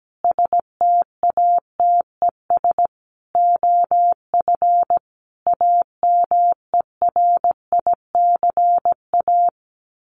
Morse code, heard as a keyed tone, about 17 wpm, 700 Hz. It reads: STATES OF AMERICA
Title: Sets of 5 Words 17wpm